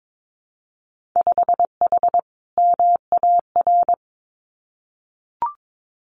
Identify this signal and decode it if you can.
Morse code, heard as a keyed tone, about 22 wpm, 700 Hz. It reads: 5H MAR
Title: Winter Field Day Exchanges Encore 22wpm